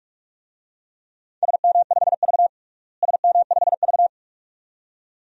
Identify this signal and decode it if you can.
Morse code, heard as a keyed tone, about 47 wpm, 700 Hz. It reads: SM5V SM5V
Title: DXCC-100 Call Signs - 4 Characters - Repeated 2x - 45wpm